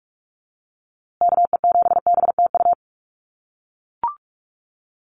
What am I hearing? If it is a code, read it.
Morse code, heard as a keyed tone, about 44 wpm, 700 Hz. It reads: KE7BTU
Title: USA Call Signs Repeated 2x 45wpm